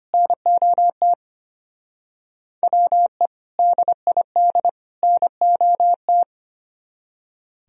Morse code, heard as a keyed tone, about 33 wpm, 700 Hz. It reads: NOT WE DID NOT